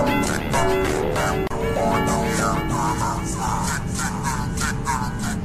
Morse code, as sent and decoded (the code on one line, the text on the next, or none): none